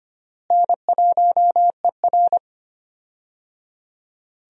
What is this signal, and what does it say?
Morse code, heard as a keyed tone, about 25 wpm, 700 Hz. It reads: N1ER
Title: USA Call Signs - 4 Characters - Repeated 3x - 25wpm